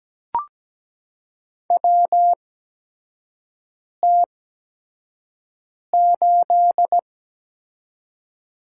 Morse code, heard as a keyed tone, about 17 wpm, spaced as for 5 wpm, 700 Hz. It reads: WT8
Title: USA Call Signs - 4 Characters - 8x Character Spacing - 17wpm